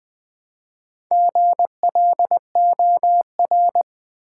Morse code, heard as a keyed tone, about 21 wpm, 700 Hz. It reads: GLOR
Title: Build A Word - 1 Letter At A Time - 3 to 5-Letter Words - 20wpm